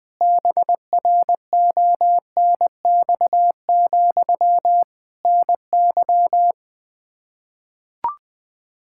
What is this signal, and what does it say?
Morse code, heard as a keyed tone, about 20 wpm, 700 Hz. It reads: BRONX, NY
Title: K1USN SST QSO Practice 20wpm